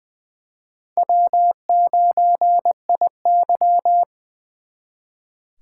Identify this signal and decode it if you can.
Morse code, heard as a keyed tone, about 23 wpm, 700 Hz. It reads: W9IY
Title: Speed Racing - USA 1x2 Call Signs 20wpm